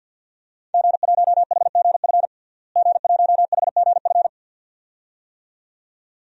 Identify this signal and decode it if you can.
Morse code, heard as a keyed tone, about 53 wpm, 700 Hz. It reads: G1HGF G1HGF
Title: DXCC-100 Call Signs - 5 Characters - Repeated 2x - 50wpm